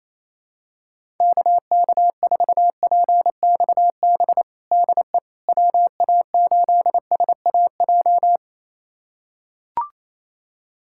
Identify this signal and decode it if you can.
Morse code, heard as a keyed tone, about 27 wpm, 700 Hz. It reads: KK4PXB DE WA8SAJ
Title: USA Callsign DE Callsign - 6 Characters - 28wpm